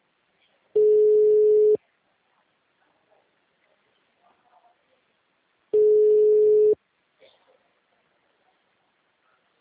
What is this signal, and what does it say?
Ringback tone (Europe, 425 Hz).